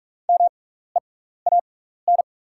Morse code, heard as a keyed tone, about 43 wpm, spaced as for 15 wpm, 690 Hz. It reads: MEAN
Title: 6x Character Spacing - Top 300 Words - 45wpm